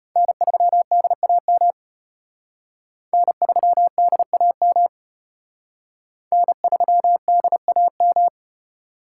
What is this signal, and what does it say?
Morse code, heard as a keyed tone, about 33 wpm, 700 Hz. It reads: N3DAM N3DAM N3DAM